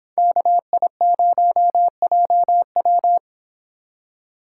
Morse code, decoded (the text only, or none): KI0JW